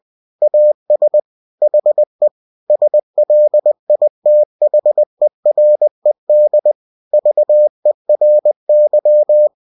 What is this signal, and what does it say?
Morse code, heard as a keyed tone, about 20 wpm, 600 Hz. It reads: AS HE SLITHERED VERY
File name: cw-books / winnie-the-pooh / winnie-20wpm-page-0004